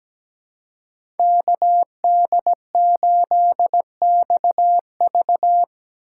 Morse code, heard as a keyed tone, about 17 wpm, 700 Hz. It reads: KD8XV
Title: USA Call Signs - 2x Character Spacing - 17wpm